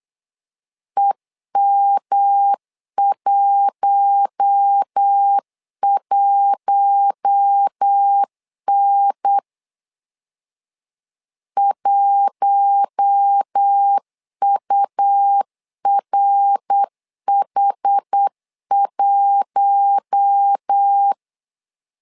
Morse code, heard as a keyed tone, about 8 wpm, 800 Hz. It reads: EM11N 1URH1
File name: LEZIONI / 4 Q1 8wpm 3035